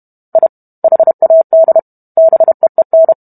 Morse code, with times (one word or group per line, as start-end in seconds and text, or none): 0.4-0.5 I
0.8-1.8 HAD
2.2-3.1 BEEN